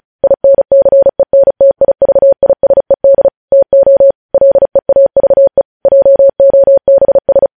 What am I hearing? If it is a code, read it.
Morse code, heard as a keyed tone, about 36 wpm, 550 Hz. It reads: INCENTIVISED TO LEAVE JOBS
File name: podcast / News headlines-1x-35x35